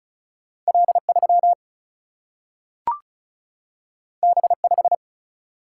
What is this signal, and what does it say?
Morse code, heard as a keyed tone, about 36 wpm, 700 Hz. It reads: L3 B5
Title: Single-Letter Single-Number 35wpm